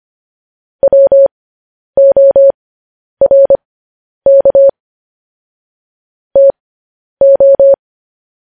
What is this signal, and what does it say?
Morse code, heard as a keyed tone, about 25 wpm, spaced as for 10 wpm, 550 Hz. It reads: WORK TO
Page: news headlines in morse code at 10 wpm